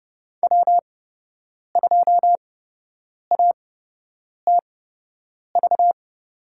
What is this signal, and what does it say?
Morse code, heard as a keyed tone, about 32 wpm, spaced as for 8 wpm, 700 Hz. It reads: W2ATV
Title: USA Call Signs - 8x Character Spacing - 30wpm